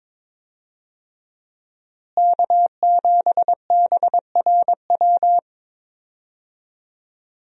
Morse code, heard as a keyed tone, about 22 wpm, 700 Hz. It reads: K7BRW
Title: USA Call Signs - 5 Characters - Repeated 3x - 22wpm